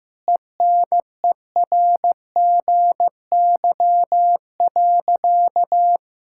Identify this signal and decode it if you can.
Morse code, heard as a keyed tone, about 15 wpm, 700 Hz. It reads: ENERGY.
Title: Sentences from Top 1000 Words - Review Part 6 - 15wpm